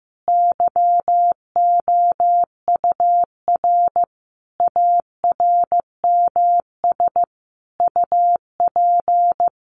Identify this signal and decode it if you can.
Morse code, heard as a keyed tone, about 15 wpm, 700 Hz. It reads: YOUR ARMS UP
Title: A Sound Of Thunder 15wpm